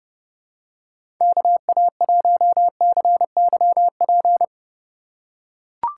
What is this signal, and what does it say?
Morse code, heard as a keyed tone, about 30 wpm, 700 Hz. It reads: KA1CYP
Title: USA Call Signs - 6 Characters - 6x Character Spacing - 30wpm